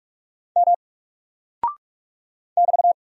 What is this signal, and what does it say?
Morse code, heard as a keyed tone, about 46 wpm, 700 Hz. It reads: M <BT>
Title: Single Character (Letter:Number:Punctuation:Prosign) Alt 45wpm